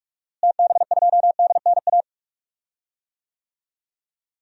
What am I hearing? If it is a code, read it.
Morse code, heard as a keyed tone, about 46 wpm, 700 Hz. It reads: TB2DNA
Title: DXCC-100 Call Signs - 6 Characters - Repeated 2x - 45wpm